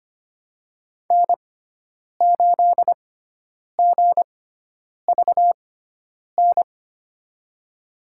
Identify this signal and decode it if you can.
Morse code, heard as a keyed tone, about 25 wpm, spaced as for 9 wpm, 700 Hz. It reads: N8GVN
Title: USA Call Signs - 5 Characters - 6x Character Spacing - 25wpm